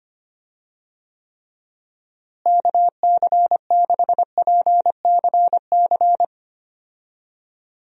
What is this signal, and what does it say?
Morse code, heard as a keyed tone, about 25 wpm, 700 Hz. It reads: KC6PCC
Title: USA Call Signs - 6 Characters - Repeated 3x - 25wpm